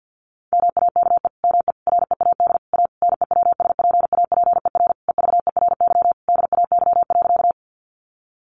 Morse code, heard as a keyed tone, about 50 wpm, 700 Hz. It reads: MAKE ME READ A NEWSPAPER EVERY DAY.
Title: Sentences from Top 1100 Words 50wpm